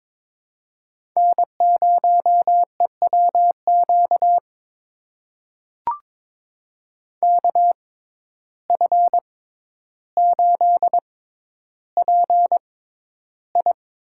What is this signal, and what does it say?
Morse code, heard as a keyed tone, about 22 wpm, spaced as for 11 wpm, 700 Hz. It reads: N0EWQ KF8PI
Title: USA Call Signs - 5 Characters - 6x Character Spacing - 22wpm